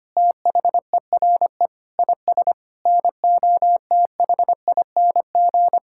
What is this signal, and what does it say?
Morse code, heard as a keyed tone, about 25 wpm, 700 Hz. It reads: THERE IS NOTHING
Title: Sentences from Top 1000 Words - Review Part 6 - 25wpm